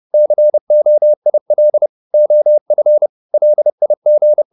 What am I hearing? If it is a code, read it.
Morse code, heard as a keyed tone, about 30 wpm, 600 Hz. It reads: COIL OF LIG